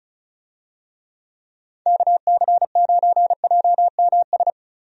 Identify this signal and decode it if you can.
Morse code, heard as a keyed tone, about 34 wpm, 700 Hz. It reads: KC9JMS
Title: USA Call Signs - 6 Characters - Repeated 2x - 35wpm